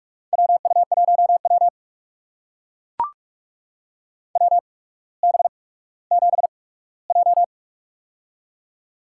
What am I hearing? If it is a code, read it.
Morse code, heard as a keyed tone, about 44 wpm, spaced as for 19 wpm, 700 Hz. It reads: WU1W WB7J
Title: USA Call Signs - 4 Characters - 8x Character Spacing - 45wpm